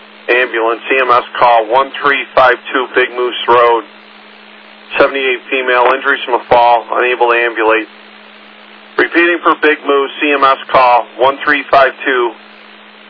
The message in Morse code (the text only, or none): none